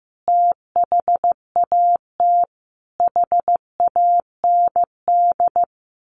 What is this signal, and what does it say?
Morse code, heard as a keyed tone, about 15 wpm, 700 Hz. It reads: THAT HAND
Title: A Sound Of Thunder 15wpm